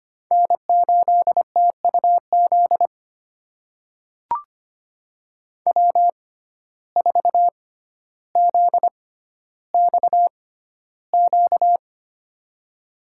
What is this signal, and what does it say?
Morse code, heard as a keyed tone, about 25 wpm, spaced as for 13 wpm, 700 Hz. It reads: N8TUZ W4ZXQ
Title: USA Call Signs - 5 Characters - 6x Character Spacing - 25wpm